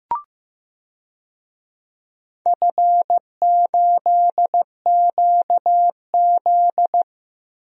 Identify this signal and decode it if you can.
Morse code, heard as a keyed tone, about 15 wpm, 700 Hz. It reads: F8QZ
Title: DXCC-100 Call Signs - 4 Characters - Repeated 2x - 15wpm